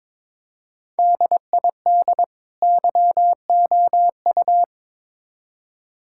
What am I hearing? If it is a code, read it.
Morse code, heard as a keyed tone, about 22 wpm, 700 Hz. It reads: DID YOU
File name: Build A Phrase - 2-Word Phrases - 22wpm